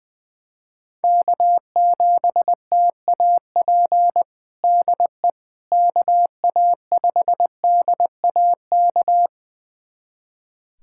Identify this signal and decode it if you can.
Morse code, heard as a keyed tone, about 20 wpm, 700 Hz. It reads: K7TAP DE KA5DAK
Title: USA Callsign DE Callsign 20wpm